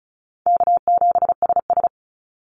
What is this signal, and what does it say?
Morse code, heard as a keyed tone, about 34 wpm, 700 Hz. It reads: K7SS
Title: Easy Call Signs Repeated 2x - 35wpm